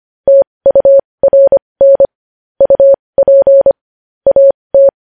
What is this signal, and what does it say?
Morse code, heard as a keyed tone, about 25 wpm, spaced as for 20 wpm, 550 Hz. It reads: TURN UP AT